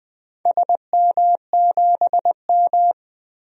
Morse code, heard as a keyed tone, about 21 wpm, 700 Hz. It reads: SM7M
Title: DXCC-100 Call Signs - 4 Characters - 20wpm